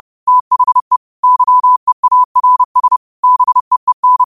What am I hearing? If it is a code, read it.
Morse code, heard as a keyed tone, about 30 wpm, 1000 Hz. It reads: THE YEARS BEEN
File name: qotdfiles / 30wpm